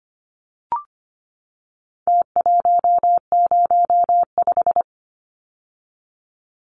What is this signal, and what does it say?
Morse code, heard as a keyed tone, about 25 wpm, 700 Hz. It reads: T105
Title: Numbers - Contest Style - 4-Digit 25wpm